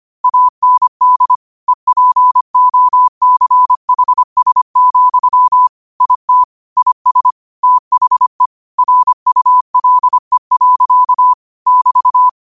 Morse code, heard as a keyed tone, about 25 wpm, 1000 Hz. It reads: AND EPOCHS, IT IS THE RULE. <BT>
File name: qotdfiles / 25wpm